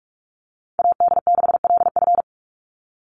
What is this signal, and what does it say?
Morse code, heard as a keyed tone, about 44 wpm, 700 Hz. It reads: AD6LF
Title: USA Call Signs 45wpm